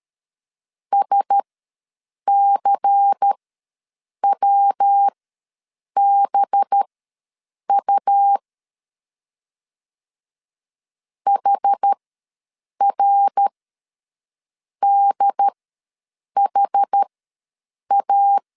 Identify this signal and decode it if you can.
Morse code, heard as a keyed tone, about 13 wpm, spaced as for 7 wpm, 800 Hz. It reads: SCWBU HRDHA
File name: LEZIONI / 5 LBVC / 5 LBVC ws 12wpm 3159